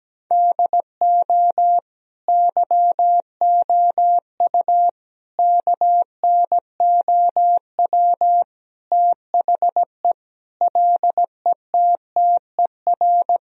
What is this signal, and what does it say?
Morse code, heard as a keyed tone, about 17 wpm, 700 Hz. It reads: DO YOU KNOW THE LETTER
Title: Sentences from Top 1000 Words - Review Part 2 - 17wpm